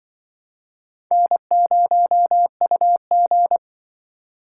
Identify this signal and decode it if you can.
Morse code, heard as a keyed tone, about 25 wpm, 700 Hz. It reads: N0UG
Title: Speed Racing - USA 1x2 Call Signs 20wpm